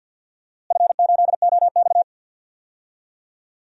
Morse code, heard as a keyed tone, about 48 wpm, 700 Hz. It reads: F8OX
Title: DXCC-100 Call Signs - 4 Characters - Repeated 2x - 50wpm